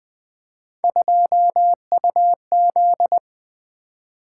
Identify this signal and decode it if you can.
Morse code, heard as a keyed tone, about 20 wpm, 700 Hz. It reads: 2UZ
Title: Three Letter-Number Combinations - 20wpm